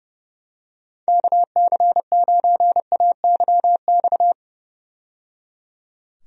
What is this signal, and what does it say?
Morse code, heard as a keyed tone, about 31 wpm, 700 Hz. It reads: KC9AYX
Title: USA Call Signs - 6 Characters - 30wpm